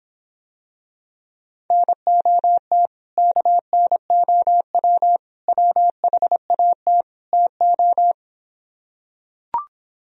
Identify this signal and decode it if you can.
Morse code, heard as a keyed tone, about 26 wpm, 700 Hz. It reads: NOT KNOW WHAT TO